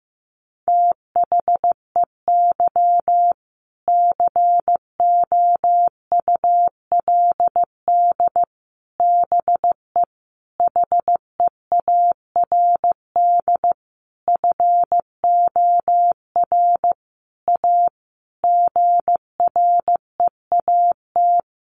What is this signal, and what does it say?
Morse code, heard as a keyed tone, about 15 wpm, 700 Hz. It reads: THEY COULD BE HEARD FOR A GREAT
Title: Sentences from Top 1200 Words 15wpm